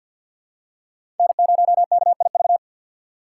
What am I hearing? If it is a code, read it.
Morse code, heard as a keyed tone, about 51 wpm, 700 Hz. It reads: N0KIV
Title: USA Call Signs - 5 Characters - 2x Character Spacing - 50wpm